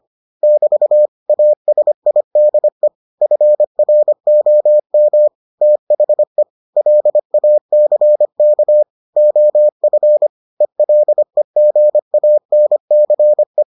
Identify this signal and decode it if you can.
Morse code, heard as a keyed tone, about 25 wpm, 600 Hz. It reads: <BT> ASIDE FROM THE LACK OF ELEGANCE